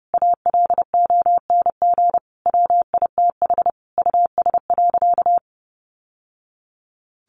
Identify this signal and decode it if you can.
Morse code, heard as a keyed tone, about 30 wpm, 700 Hz. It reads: ALONG WITH US.
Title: Sentences from Top 1200 Words 30wpm